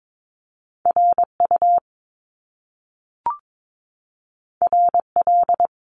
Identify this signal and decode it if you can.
Morse code, heard as a keyed tone, about 22 wpm, 700 Hz. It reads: RU RL
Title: Top Two Letter Combinations in Words 22wpm